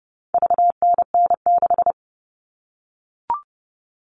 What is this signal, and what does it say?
Morse code, heard as a keyed tone, about 30 wpm, 700 Hz. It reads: VNN6
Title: All Cut Numbers - 4-Digit 30wpm